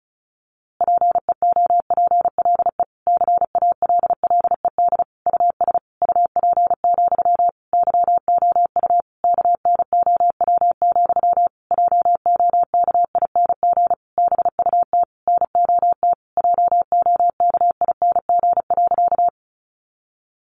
Morse code, heard as a keyed tone, about 34 wpm, 700 Hz. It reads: PEOPLE CALLED US UP, YOU KNOW, JOKING BUT NOT JOKING.
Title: A Sound Of Thunder 35wpm